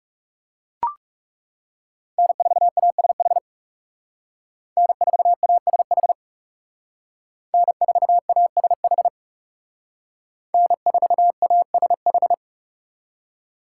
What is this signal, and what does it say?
Morse code, heard as a keyed tone, about 39 wpm, 700 Hz. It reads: N4ASH N4ASH N4ASH N4ASH